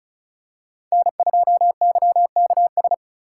Morse code, heard as a keyed tone, about 34 wpm, 700 Hz. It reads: N2YKS
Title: USA Call Signs - 5 Characters - 2x Character Spacing - 35wpm